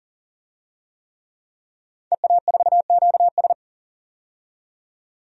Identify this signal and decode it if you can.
Morse code, heard as a keyed tone, about 40 wpm, 700 Hz. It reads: EA4QS
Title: DXCC-100 Call Signs - 5 Characters - 40wpm